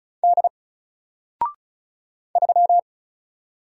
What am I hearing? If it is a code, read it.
Morse code, heard as a keyed tone, about 36 wpm, 700 Hz. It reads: D 3
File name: Single-Letter-Number-35wpm